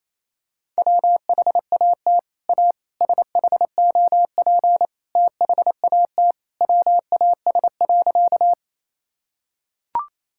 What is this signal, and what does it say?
Morse code, heard as a keyed tone, about 28 wpm, 700 Hz. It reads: WHAT A SHOP THAT WAS.
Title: Sentences from Top 1600 Words 28wpm